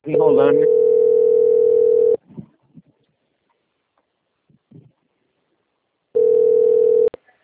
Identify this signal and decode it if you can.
Ringback tone (North America), 440 + 480 Hz, with speech.
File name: out- - - - - .872